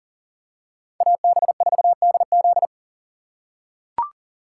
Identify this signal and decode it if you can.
Morse code, heard as a keyed tone, about 38 wpm, 690 Hz. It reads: AB4DZ
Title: USA Call Signs - 6x Character Spacing - 40wpm